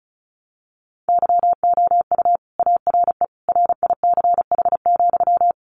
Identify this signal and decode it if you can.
Morse code, heard as a keyed tone, about 34 wpm, 700 Hz. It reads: YOU ARE RICH,
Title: Sentences from Top 1100 Words 35wpm